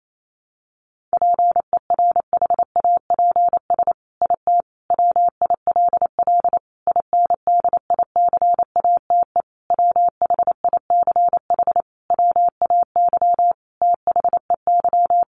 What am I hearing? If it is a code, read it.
Morse code, heard as a keyed tone, about 28 wpm, 700 Hz. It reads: PERHAPS IT WILL INDICATE WHICH WAY THEY
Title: Sentences from Top 1300 Words 28wpm